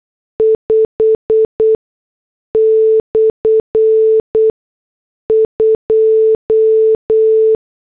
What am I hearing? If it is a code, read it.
Morse code, heard as a keyed tone, about 8 wpm, spaced as for 6 wpm, 440 Hz. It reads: 5/2
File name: morse / headlines